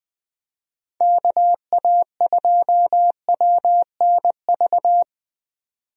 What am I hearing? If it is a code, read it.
Morse code, heard as a keyed tone, about 20 wpm, 700 Hz. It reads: KA2WNV
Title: USA Call Signs - 6 Characters - 20wpm